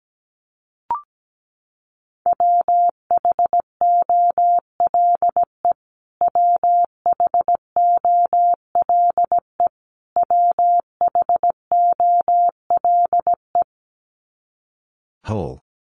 Morse code, heard as a keyed tone, about 17 wpm, 700 Hz. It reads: WHOLE WHOLE WHOLE
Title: Top 500 Words Repeated 17wpm